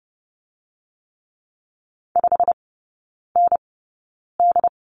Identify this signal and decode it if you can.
Morse code, heard as a keyed tone, about 30 wpm, spaced as for 9 wpm, 700 Hz. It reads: 5ND
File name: Lesson 17e - Letter u - Three Characters 30wpm